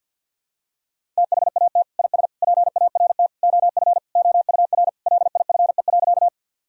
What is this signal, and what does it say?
Morse code, heard as a keyed tone, about 51 wpm, 700 Hz. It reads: THAT IS PART OF OUR LIFE.
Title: Sentences from Top 1000 Words - Review Part 1 - 50wpm